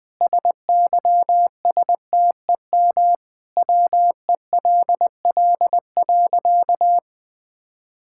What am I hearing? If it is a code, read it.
Morse code, heard as a keyed tone, about 20 wpm, 700 Hz. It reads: SYSTEM WELL.
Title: Sentences from Top 1000 Words - Review Part 5 - 20wpm